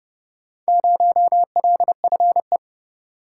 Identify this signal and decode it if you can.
Morse code, heard as a keyed tone, about 29 wpm, 700 Hz. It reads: 0LFE